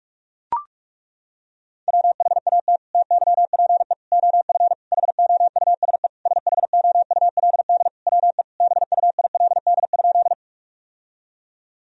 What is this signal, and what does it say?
Morse code, heard as a keyed tone, about 42 wpm, 700 Hz. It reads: WHAT TYPE OF HOUSE SHOULD WE BUILD?
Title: Sentences from Top 1000 Words - Review Part 3 - 45wpm